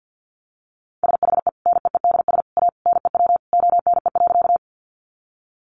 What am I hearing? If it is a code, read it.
Morse code, heard as a keyed tone, about 49 wpm, 700 Hz. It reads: SHE NEEDS A NEW ONE.
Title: Additional Easy Phrases 50wpm